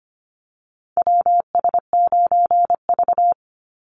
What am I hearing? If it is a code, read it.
Morse code, heard as a keyed tone, about 25 wpm, 700 Hz. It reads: WS9V
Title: Call Signs Repeated 2x - 25wpm